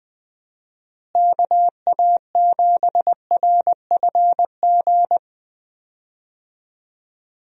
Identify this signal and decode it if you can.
Morse code, heard as a keyed tone, about 20 wpm, 700 Hz. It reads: KA7RFG
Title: USA Call Signs - 6 Characters - Repeated 2x - 20wpm